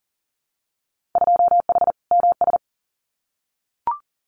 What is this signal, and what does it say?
Morse code, heard as a keyed tone, about 40 wpm, 700 Hz. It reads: 2H MS
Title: Winter Field Day Exchanges 40wpm